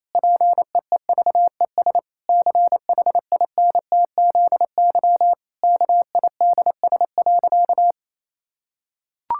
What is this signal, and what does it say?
Morse code, heard as a keyed tone, about 28 wpm, 700 Hz. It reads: PEEVES CHINTZY KIDS.
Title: Pangram Challenge v2 - 28wpm